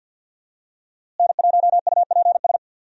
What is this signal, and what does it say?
Morse code, heard as a keyed tone, about 46 wpm, 700 Hz. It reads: N1UPS
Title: USA Call Signs - 5 Characters - Repeated 3x - 50wpm